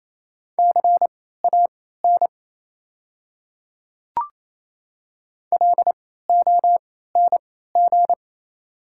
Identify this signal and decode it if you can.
Morse code, heard as a keyed tone, about 28 wpm, spaced as for 16 wpm, 700 Hz. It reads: CAN LONG
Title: 3x Character Spacing - Top 200 Words - 28wpm